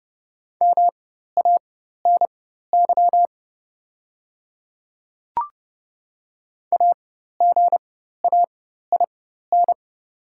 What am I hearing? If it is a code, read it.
Morse code, heard as a keyed tone, about 30 wpm, spaced as for 14 wpm, 700 Hz. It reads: MANY AGAIN
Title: 4x Character Spacing - Top 200 Words - 30wpm